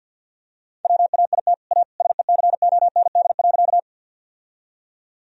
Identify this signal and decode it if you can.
Morse code, heard as a keyed tone, about 51 wpm, 700 Hz. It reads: WAIT A SECOND.
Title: Sentences from Top 1000 Words - Review Part 1 - 50wpm